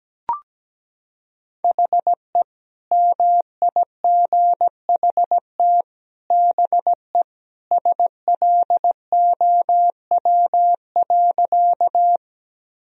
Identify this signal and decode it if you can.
Morse code, heard as a keyed tone, about 17 wpm, 700 Hz. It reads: HE MIGHT BE SLOW.
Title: Sentences from Top 1000 Words - Review Part 2 - 17wpm